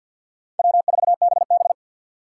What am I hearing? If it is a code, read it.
Morse code, heard as a keyed tone, about 50 wpm, 700 Hz. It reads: W4BB
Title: USA Call Signs - 6x Character Spacing - 50wpm